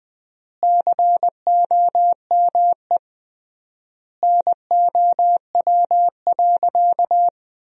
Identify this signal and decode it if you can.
Morse code, heard as a keyed tone, about 20 wpm, 700 Hz. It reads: COME NOW.
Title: 3x Word-Spacing - Sentences from Top 200 Words - 20wpm